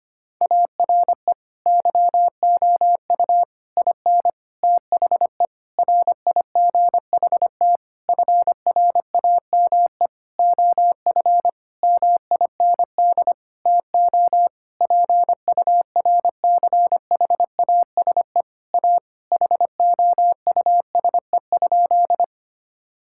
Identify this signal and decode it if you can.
Morse code, heard as a keyed tone, about 25 wpm, 700 Hz. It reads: ARE YOU IN THE RIGHT FRAME OF MIND TO PURCHASE A HOUSE?
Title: Sentences from Top 1000 Words - Review Part 6 - 25wpm